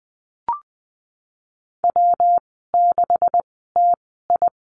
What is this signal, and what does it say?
Morse code, heard as a keyed tone, about 20 wpm, spaced as for 15 wpm, 700 Hz. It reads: W6TI
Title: 179 Lesson 29g - Slash - Call Signs 20wpm